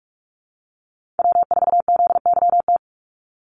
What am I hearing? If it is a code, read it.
Morse code, heard as a keyed tone, about 44 wpm, 700 Hz. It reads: W4ZYT
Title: Call Signs Repeated 3x - 45wpm